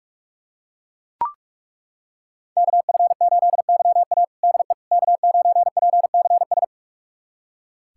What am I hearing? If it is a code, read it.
Morse code, heard as a keyed tone, about 42 wpm, 700 Hz. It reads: KF8YA DE K9PCS
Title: USA Callsign DE Callsign - 5 Characters - 45wpm